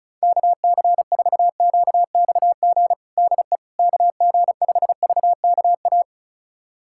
Morse code, heard as a keyed tone, about 34 wpm, 700 Hz. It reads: KC4QXG DE KG5VKA
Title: USA Callsign DE Callsign - 6 Characters - 35wpm